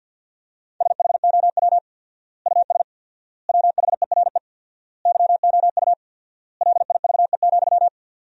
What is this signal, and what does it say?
Morse code, heard as a keyed tone, about 50 wpm, 700 Hz. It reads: SHOW US WHERE YOU LIVE,
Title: 4x Word-Spacing - Sentences from Top 200 Words - 50wpm